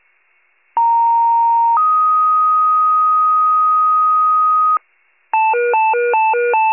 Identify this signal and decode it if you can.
Two-tone page, heard as a one-second tone, then a three-second tone, about 930 then 1250 Hz. It is heard from 0.8 to 4.8 s.
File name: pager / 94 RESCUE OFC / wcalert